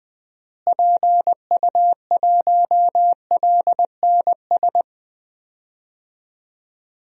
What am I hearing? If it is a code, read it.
Morse code, heard as a keyed tone, about 20 wpm, 700 Hz. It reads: PU1LNS
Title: DXCC-100 Call Signs - 6 Characters - Repeated 3x - 20wpm